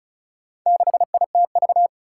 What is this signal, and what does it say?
Morse code, heard as a keyed tone, about 36 wpm, 700 Hz. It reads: 6ITV
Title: Four Challenging Letter-Number Combinations 35wpm